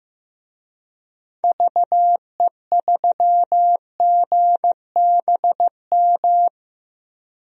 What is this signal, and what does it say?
Morse code, heard as a keyed tone, about 15 wpm, 700 Hz. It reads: VE3GBM
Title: DXCC-100 Call Signs - 6 Characters - 15wpm